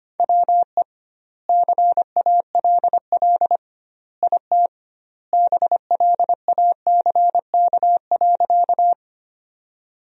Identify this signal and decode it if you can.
Morse code, heard as a keyed tone, about 25 wpm, 700 Hz. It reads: WE CALL IT BLACK.
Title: 2x Word-Spacing - Sentences from Top 400 Words - 25wpm